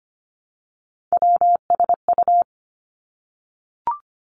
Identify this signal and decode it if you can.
Morse code, heard as a keyed tone, about 25 wpm, 700 Hz. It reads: WSU
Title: Three Challenging Letter-Number Combinations 25wpm